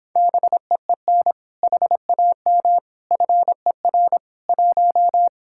Morse code, heard as a keyed tone, about 26 wpm, 700 Hz. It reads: BEEN HAM FER 1